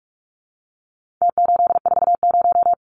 Morse code, heard as a keyed tone, about 44 wpm, 700 Hz. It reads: T840
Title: Numbers - Contest Style - 4-Digit 45wpm